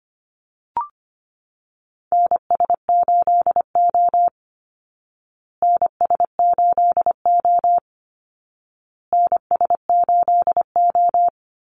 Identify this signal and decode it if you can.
Morse code, heard as a keyed tone, about 25 wpm, 700 Hz. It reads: NS8O NS8O NS8O
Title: Call Signs Repeated 2x - 25wpm